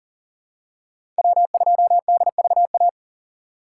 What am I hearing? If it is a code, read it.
Morse code, heard as a keyed tone, about 40 wpm, 700 Hz. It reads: W2DVA
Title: USA Call Signs - 5 Characters - 3x Character Spacing - 40wpm